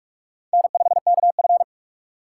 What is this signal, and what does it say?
Morse code, heard as a keyed tone, about 46 wpm, 700 Hz. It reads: N5KF
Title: USA Call Signs - 4 Characters - 6x Character Spacing - 45wpm